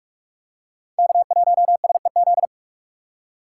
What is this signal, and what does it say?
Morse code, heard as a keyed tone, about 41 wpm, 700 Hz. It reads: K1SEZ
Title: USA Call Signs - 5 Characters - 6x Character Spacing - 45wpm